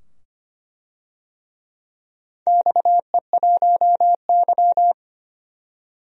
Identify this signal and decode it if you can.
Morse code, heard as a keyed tone, about 25 wpm, 700 Hz. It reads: XE1Y
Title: DXCC-100 Call Signs - 4 Characters - 25wpm